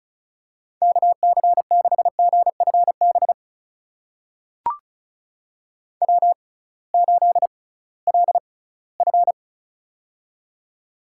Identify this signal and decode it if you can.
Morse code, heard as a keyed tone, about 34 wpm, spaced as for 20 wpm, 700 Hz. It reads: KC6GFB W8LF
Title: USA Call Signs - 6x Character Spacing - 35wpm